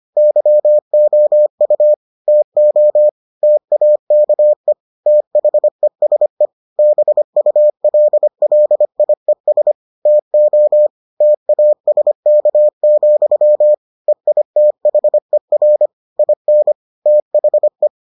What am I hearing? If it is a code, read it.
Morse code, heard as a keyed tone, about 25 wpm, 600 Hz. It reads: YOU TO TAKE THESE BULLIES TO TASK, EITHER IN THE